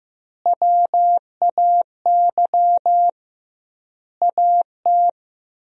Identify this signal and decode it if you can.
Morse code, heard as a keyed tone, about 15 wpm, 700 Hz. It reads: WAY AT
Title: 2x Word-Spacing - Sentences from Top 300 Words - 15wpm